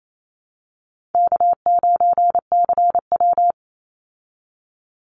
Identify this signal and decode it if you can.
Morse code, heard as a keyed tone, about 28 wpm, 700 Hz. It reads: K9CW
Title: Easy Call Signs Repeated 2x - 28wpm